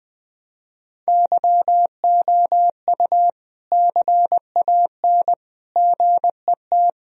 Morse code, heard as a keyed tone, about 21 wpm, 700 Hz. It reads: YOU CAN GET
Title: Sentences from Top 1000 Words - Review Part 2 - 20wpm